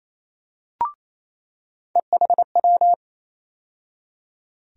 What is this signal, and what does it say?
Morse code, heard as a keyed tone, about 28 wpm, 700 Hz. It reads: EHW
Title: USA Call Sign Postfixes - 3 Letters 28wpm